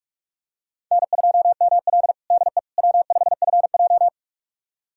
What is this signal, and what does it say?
Morse code, heard as a keyed tone, about 47 wpm, 700 Hz. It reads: N2ML DE W5FJ
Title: USA Callsign DE Callsign - 4 Characters - 45wpm